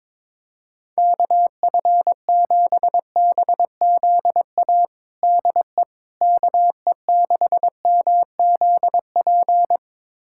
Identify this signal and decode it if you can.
Morse code, heard as a keyed tone, about 22 wpm, 700 Hz. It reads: KF7BZA DE KE6MZP
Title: USA Callsign DE Callsign - 6 Characters - 22wpm